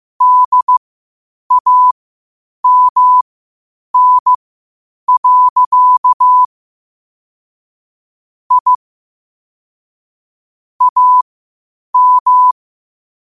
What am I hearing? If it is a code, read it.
Morse code, heard as a keyed tone, about 15 wpm, spaced as for 8 wpm, 1000 Hz. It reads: DAMN. I AM
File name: qotdfiles / 8wpm